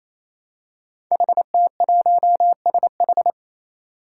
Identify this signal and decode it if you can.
Morse code, heard as a keyed tone, about 27 wpm, 700 Hz. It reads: HT1SH
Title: Five Challenging Letter-Number Combinations 28wpm